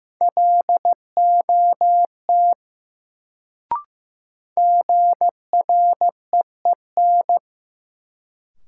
Morse code, heard as a keyed tone, about 15 wpm, 700 Hz. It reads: LOT GREEN
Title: Sentences from Top 500 Words - New Word Drill - 15wpm